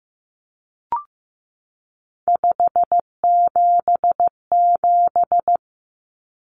Morse code, heard as a keyed tone, about 14 wpm, 700 Hz. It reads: 577
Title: Numbers - 3-Digit 15wpm